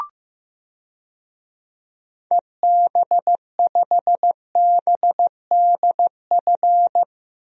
Morse code, heard as a keyed tone, about 15 wpm, 700 Hz. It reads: EB5BDF
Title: DXCC-100 Call Signs - 6 Characters - Repeated 3x - 15wpm